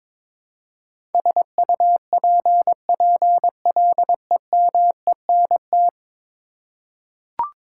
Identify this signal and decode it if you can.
Morse code, heard as a keyed tone, about 22 wpm, 700 Hz. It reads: SUPPLEMENT
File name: practice / 22wpm / 10 Letter Words - First 3 Letters Given - 22wpm